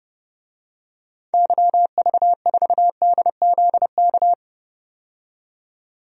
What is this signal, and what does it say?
Morse code, heard as a keyed tone, about 31 wpm, 700 Hz. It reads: YV4DZK
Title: DXCC-100 Call Signs - 6 Characters - 30wpm